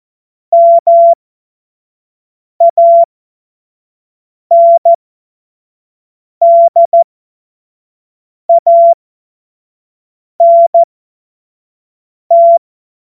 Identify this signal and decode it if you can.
Morse code, heard as a keyed tone, about 14 wpm, spaced as for 5 wpm, 680 Hz. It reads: MANDANT
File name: Morsekurs / Witze / Witz 21. Juni 2022 25 F70